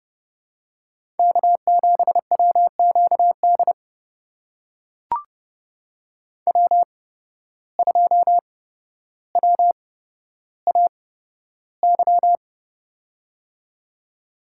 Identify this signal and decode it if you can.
Morse code, heard as a keyed tone, about 30 wpm, spaced as for 13 wpm, 700 Hz. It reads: K7WQD W2WAY
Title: USA Call Signs - 8x Character Spacing - 30wpm